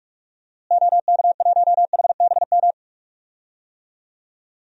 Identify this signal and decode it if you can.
Morse code, heard as a keyed tone, about 46 wpm, 700 Hz. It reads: OK1HBM
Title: DXCC-100 Call Signs - 6 Characters - Repeated 2x - 45wpm